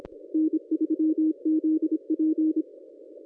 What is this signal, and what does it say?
Morse code, heard as a keyed tone, about 26 wpm, 320 Hz. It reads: N3ZP
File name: contests / ss-cw-k5tr-op-n5rz -1100